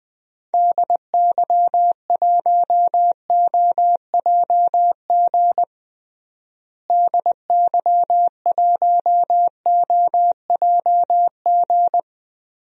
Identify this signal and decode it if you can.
Morse code, heard as a keyed tone, about 21 wpm, 700 Hz. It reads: DY1OJG DY1OJG
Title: DXCC-100 Call Signs - 6 Characters - Repeated 2x - 20wpm